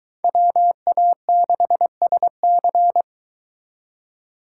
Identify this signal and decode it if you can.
Morse code, heard as a keyed tone, about 22 wpm, 700 Hz. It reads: WA6SC